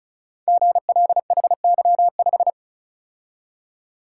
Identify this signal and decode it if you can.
Morse code, heard as a keyed tone, about 36 wpm, 700 Hz. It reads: GLHY5
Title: Code Groups - 5 Letter:Numbers - 35wpm